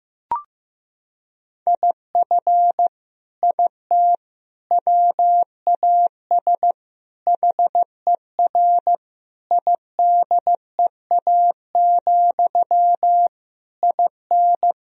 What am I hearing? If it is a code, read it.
Morse code, heard as a keyed tone, about 15 wpm, 700 Hz. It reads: IF IT WAS HER IDEA, IN
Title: Sentences from Top 1000 Words - Review Part 2 - 15wpm